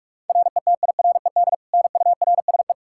Morse code, heard as a keyed tone, about 46 wpm, 700 Hz. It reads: RETIRED NURSE
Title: Practice QSO - Exchange 3 - More Info - 45wpm